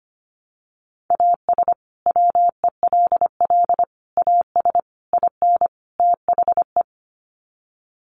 Morse code, heard as a keyed tone, about 25 wpm, 700 Hz. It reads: AS WELL AS IN THE
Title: Sets of 5 Words Encore 25wpm